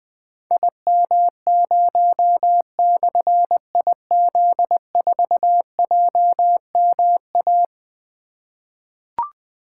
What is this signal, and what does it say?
Morse code, heard as a keyed tone, about 20 wpm, 700 Hz. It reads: IM0/IZ4JMA
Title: Difficult Call Signs 20wpm